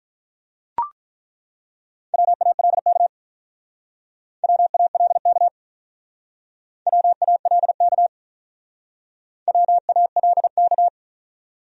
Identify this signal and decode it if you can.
Morse code, heard as a keyed tone, about 47 wpm, 700 Hz. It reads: WALK WALK WALK WALK